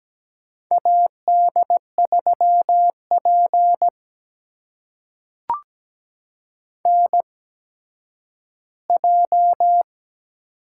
Morse code, heard as a keyed tone, about 17 wpm, spaced as for 10 wpm, 700 Hz. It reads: AD3P NJ
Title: USA Call Signs - 4 Characters - 8x Character Spacing - 17wpm